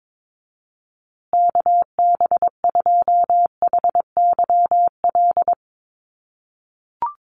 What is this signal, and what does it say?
Morse code, heard as a keyed tone, about 22 wpm, 700 Hz. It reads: KB2HYL
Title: USA Call Signs Repeated 2x 22wpm